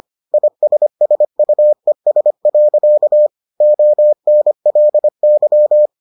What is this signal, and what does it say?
Morse code, heard as a keyed tone, about 25 wpm, 600 Hz. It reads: ISSUES. ONLY